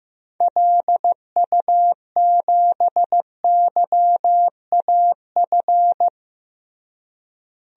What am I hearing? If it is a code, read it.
Morse code, heard as a keyed tone, about 15 wpm, 700 Hz. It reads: LU7YAF
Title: DXCC-100 Call Signs - 6 Characters - Repeated 3x - 15wpm